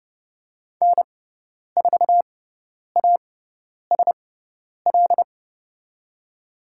Morse code, heard as a keyed tone, about 30 wpm, spaced as for 10 wpm, 700 Hz. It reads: N4ASL
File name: Lesson 16g - Letter c - Call Signs 30wpm